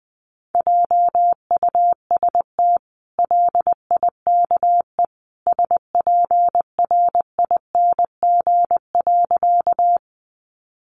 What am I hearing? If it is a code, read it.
Morse code, heard as a keyed tone, about 20 wpm, 700 Hz. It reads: JUST LIKE SPRING.
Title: Sentences from Top 1200 Words 20wpm